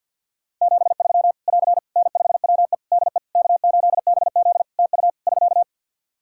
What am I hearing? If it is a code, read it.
Morse code, heard as a keyed tone, about 50 wpm, 700 Hz. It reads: 73 <AR> N5WE DE K8BZ TU <SK>
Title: K1USN SST QSO Practice 50wpm